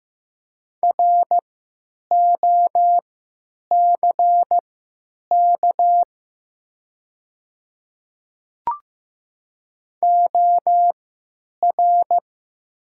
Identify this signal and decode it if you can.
Morse code, heard as a keyed tone, about 15 wpm, spaced as for 9 wpm, 700 Hz. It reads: ROCK OR
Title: 3x Character Spacing - Top 500 Words - 15wpm